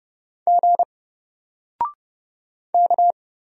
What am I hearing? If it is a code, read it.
Morse code, heard as a keyed tone, about 30 wpm, 700 Hz. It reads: G K
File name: Single-Letter-Number-30wpm